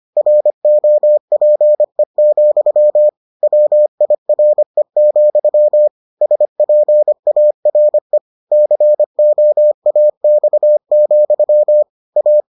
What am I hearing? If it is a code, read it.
Morse code, heard as a keyed tone, about 25 wpm, 600 Hz. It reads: ROPE, WIRE, SPARE COAX, A